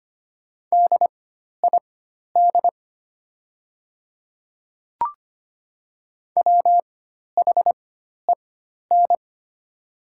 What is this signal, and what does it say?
Morse code, heard as a keyed tone, about 25 wpm, spaced as for 12 wpm, 700 Hz. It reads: DID WHEN
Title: 4x Character Spacing - Top 100 Words - 25wpm